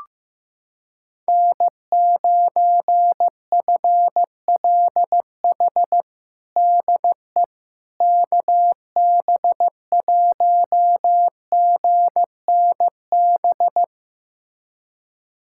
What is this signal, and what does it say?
Morse code, heard as a keyed tone, about 15 wpm, 700 Hz. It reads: N9FLH DE KB1GNB
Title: USA Callsign DE Callsign 15wpm